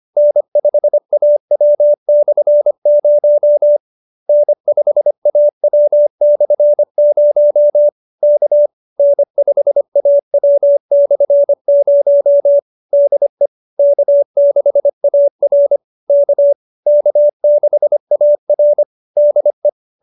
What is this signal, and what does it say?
Morse code, heard as a keyed tone, about 25 wpm, 590 Hz. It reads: N5AW/0 N5AW/0 K N5AW/0 DE K6AR K K6AR DE
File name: QSO312 25